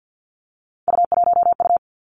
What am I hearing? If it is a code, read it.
Morse code, heard as a keyed tone, about 50 wpm, 700 Hz. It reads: U1U
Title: Three Challenging Letter-Number Combinations 50wpm